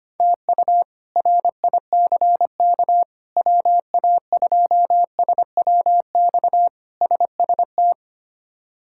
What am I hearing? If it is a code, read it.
Morse code, heard as a keyed tone, about 25 wpm, 700 Hz. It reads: TU RICK WA2SWX SST
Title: K1USN SST QSO Practice - 25wpm